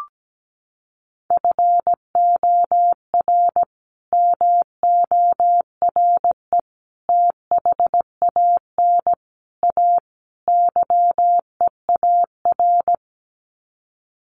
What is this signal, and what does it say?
Morse code, heard as a keyed tone, about 17 wpm, 700 Hz. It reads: FOR MORE THAN A YEAR
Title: Sets of 5 Words 17wpm